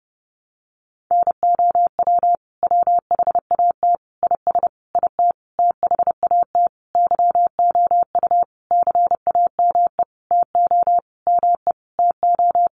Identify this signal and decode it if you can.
Morse code, heard as a keyed tone, about 30 wpm, 700 Hz. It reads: NOW WHAT IS IT THAT YOU CAME TO ME TO
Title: Sentences from Top 200 Words 30wpm